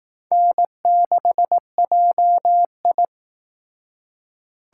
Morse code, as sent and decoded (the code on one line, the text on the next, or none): -. -.... .--- ..
N6JI